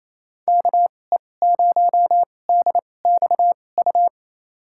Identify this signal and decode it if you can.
Morse code, heard as a keyed tone, about 27 wpm, spaced as for 20 wpm, 700 Hz. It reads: KE0DXU
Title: USA Call Signs - 6 Characters - 2x Character Spacing - 28wpm